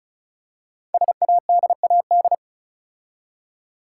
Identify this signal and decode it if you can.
Morse code, heard as a keyed tone, about 36 wpm, 700 Hz. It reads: SADAD
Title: Code Groups - 5 Letters - 35wpm